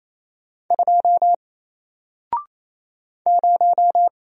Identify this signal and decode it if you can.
Morse code, heard as a keyed tone, about 27 wpm, 700 Hz. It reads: 2 0
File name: Single-Letter-Number-28wpm